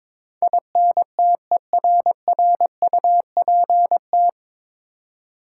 Morse code, heard as a keyed tone, about 22 wpm, 700 Hz. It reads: INTERRUPT
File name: 9 Letter Words - First 3 Letters Given - 22wpm